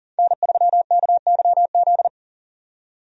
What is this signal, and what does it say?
Morse code, heard as a keyed tone, about 40 wpm, 700 Hz. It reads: N3KYZ
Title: USA Call Signs - 5 Characters - 40wpm